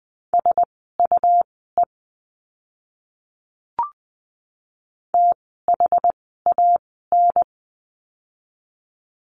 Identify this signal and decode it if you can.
Morse code, heard as a keyed tone, about 20 wpm, spaced as for 15 wpm, 700 Hz. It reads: SUE THAN
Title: 101 Lesson 17f - Letter u - Words 20wpm